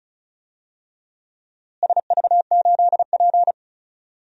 Morse code, heard as a keyed tone, about 34 wpm, 700 Hz. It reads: SV8P
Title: DXCC-100 Call Signs - 4 Characters - Repeated 2x - 35wpm